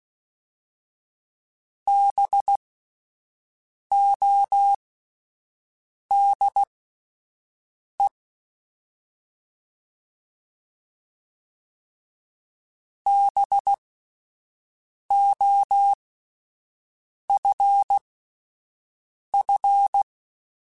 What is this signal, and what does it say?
Morse code, heard as a keyed tone, about 16 wpm, spaced as for 5 wpm, 800 Hz. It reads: BODE BOFF
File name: Spaced Alphas / Spaced Alphas mp3s / Bs sp @10